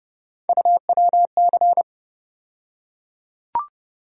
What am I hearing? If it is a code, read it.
Morse code, heard as a keyed tone, about 30 wpm, 700 Hz. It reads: UWC
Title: USA Call Sign Postfixes - 3 Letters 30wpm